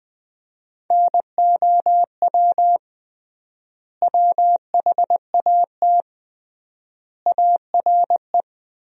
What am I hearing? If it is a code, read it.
Morse code, heard as a keyed tone, about 20 wpm, 700 Hz. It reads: NOW WHAT ARE
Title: 3x Word-Spacing - Sentences from Top 100 Words - 20wpm